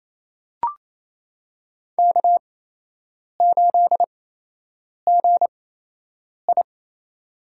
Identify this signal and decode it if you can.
Morse code, heard as a keyed tone, about 28 wpm, spaced as for 8 wpm, 700 Hz. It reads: K8GI
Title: USA Call Signs - 4 Characters - 8x Character Spacing - 28wpm